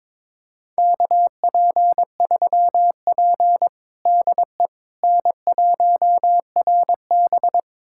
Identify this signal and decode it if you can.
Morse code, heard as a keyed tone, about 22 wpm, 700 Hz. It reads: KP3P DE N1RB
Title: USA Callsign DE Callsign - 4 Characters - 22wpm